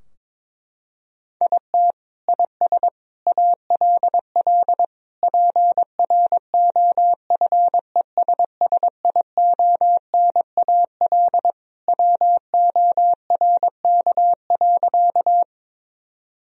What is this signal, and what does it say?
Morse code, heard as a keyed tone, about 22 wpm, 700 Hz. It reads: IT IS ALL PROFESSIONAL WORK.
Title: Sentences from Top 1000 Words - Review Part 4 - 22wpm